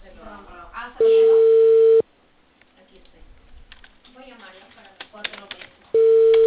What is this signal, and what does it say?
Ringback tone (China, 450 Hz), with speech.